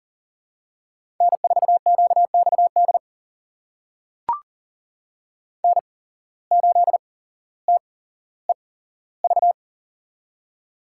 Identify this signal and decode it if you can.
Morse code, heard as a keyed tone, about 38 wpm, spaced as for 17 wpm, 700 Hz. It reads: N4QXD N8TEV
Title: USA Call Signs - 5 Characters - 8x Character Spacing - 40wpm